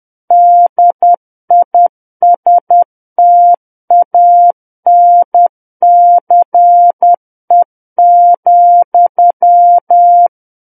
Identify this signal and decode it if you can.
Morse code, heard as a keyed tone, about 10 wpm, 700 Hz. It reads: DISTANCE,
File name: Book One / 01-Chapter 1 - 10 WPM